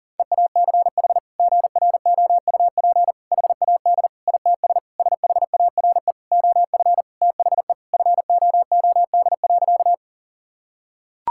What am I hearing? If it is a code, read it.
Morse code, heard as a keyed tone, about 40 wpm, 700 Hz. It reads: EACH GROUP HAD ITS SHARE OF THE FOOD.
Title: Sentences from Top 1000 Words - Review Part 6 - 40wpm